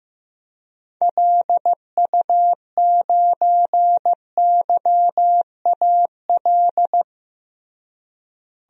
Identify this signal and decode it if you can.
Morse code, heard as a keyed tone, about 14 wpm, 700 Hz. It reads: LU9YAL
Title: DXCC-100 Call Signs - 6 Characters - Repeated 3x - 15wpm